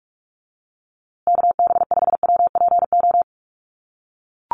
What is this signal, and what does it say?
Morse code, heard as a keyed tone, about 44 wpm, 700 Hz. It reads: KB5WPO
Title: USA Call Signs Repeated 2x 45wpm